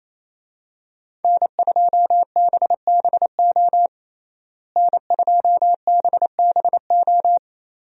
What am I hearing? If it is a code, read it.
Morse code, heard as a keyed tone, about 28 wpm, 700 Hz. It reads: N2BBO N2BBO
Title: USA Call Signs - 5 Characters - Repeated 3x - 28wpm